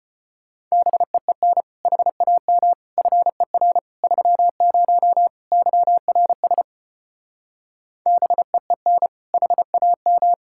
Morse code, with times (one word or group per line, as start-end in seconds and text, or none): 0.7-1.6 BEEN
1.9-2.7 HAM
3.0-3.8 FER
4.0-5.3 30
5.5-6.6 YRS
8.1-9.1 BEEN
9.3-10.3 HAM